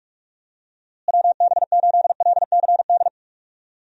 Morse code, heard as a keyed tone, about 44 wpm, 700 Hz. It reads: WB8LCD
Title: USA Call Signs - 6 Characters - Repeated 2x - 45wpm